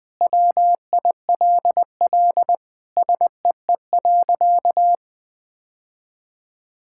Morse code, heard as a keyed tone, about 20 wpm, 700 Hz. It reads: WILL SEE.
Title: Sentences with 3-Word Phrases - 20wpm